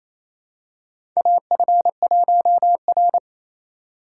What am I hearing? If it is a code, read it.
Morse code, heard as a keyed tone, about 27 wpm, 700 Hz. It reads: AF1R
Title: USA Call Signs - 4 Characters - 4x Character Spacing - 28wpm